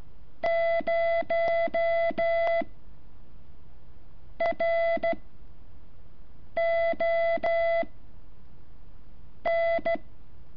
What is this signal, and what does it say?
Morse code, heard as a keyed tone, about 14 wpm, 670 Hz. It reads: 0 R O N
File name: curso / ejercicio 11